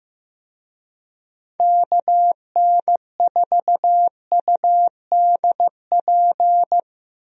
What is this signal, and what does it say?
Morse code, heard as a keyed tone, about 15 wpm, 700 Hz. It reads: KN4UDP
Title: USA Call Signs - 6 Characters - Repeated 3x - 15wpm